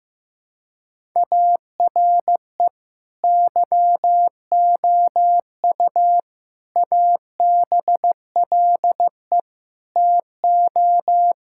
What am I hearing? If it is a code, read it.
Morse code, heard as a keyed tone, about 15 wpm, 700 Hz. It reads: ARE YOU ABLE TO
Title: Sentences from Top 1000 Words - Review Part 5 - 15wpm